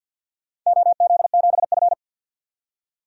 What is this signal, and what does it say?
Morse code, heard as a keyed tone, about 51 wpm, 700 Hz. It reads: OZ7F
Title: DXCC-100 Call Signs - 4 Characters - 50wpm